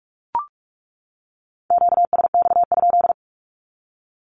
Morse code, heard as a keyed tone, about 44 wpm, 700 Hz. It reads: QSX?
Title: Q-Codes 45wpm